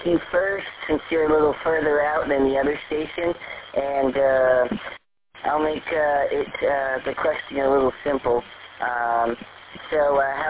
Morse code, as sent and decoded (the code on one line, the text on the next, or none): none